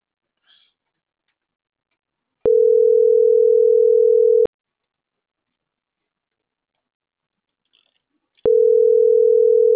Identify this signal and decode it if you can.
Ringback tone (North America), 440 + 480 Hz.